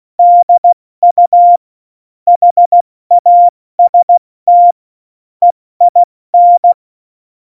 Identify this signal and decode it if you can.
Morse code, heard as a keyed tone, about 16 wpm, 700 Hz. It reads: DU HAST EIN
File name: Morsekurs / Witze / Witz 26. September 17, Tempo 70